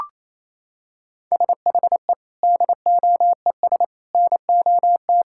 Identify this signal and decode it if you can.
Morse code, heard as a keyed tone, about 28 wpm, 700 Hz. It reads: SHE DOES NOT
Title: Sentences from Top 1000 Words - Review Part 5 - 28wpm